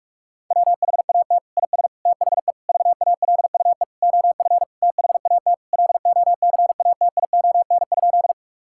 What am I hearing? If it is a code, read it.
Morse code, heard as a keyed tone, about 46 wpm, 700 Hz. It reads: WHAT IS THE VALUE OF THAT LOCATION?
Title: Sentences from Top 1000 Words - Review Part 6 - 45wpm